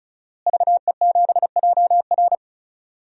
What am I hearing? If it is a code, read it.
Morse code, heard as a keyed tone, about 36 wpm, 700 Hz. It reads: VE7JR
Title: DXCC-100 Call Signs - 5 Characters - Repeated 2x - 35wpm